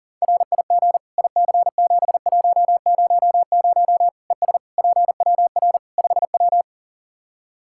Morse code, heard as a keyed tone, about 40 wpm, 700 Hz. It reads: RIG IC7100 ES PWR 5W
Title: Practice QSO - Exchange 2 - Info - 40wpm